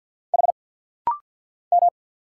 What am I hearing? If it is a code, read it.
Morse code, heard as a keyed tone, about 50 wpm, 700 Hz. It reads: H M
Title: Single Letter-Number Alt - Rapid-Fire 50wpm